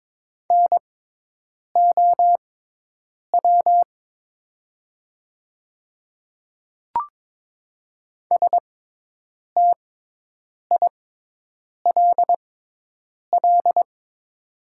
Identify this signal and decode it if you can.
Morse code, heard as a keyed tone, about 22 wpm, spaced as for 8 wpm, 700 Hz. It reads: NOW STILL
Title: 6x Character Spacing - Top 500 Words - 22wpm